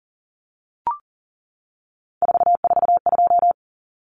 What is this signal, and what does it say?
Morse code, heard as a keyed tone, about 40 wpm, 700 Hz. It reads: 442
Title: Numbers - 3-Digit 40wpm